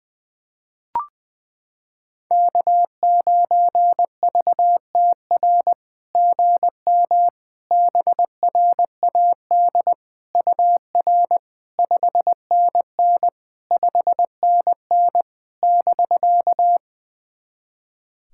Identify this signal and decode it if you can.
Morse code, heard as a keyed tone, about 20 wpm, 700 Hz. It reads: K9VTR GM BRAD UR 5NN 5NN <BK>
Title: SOTA QSO Practice 20wpm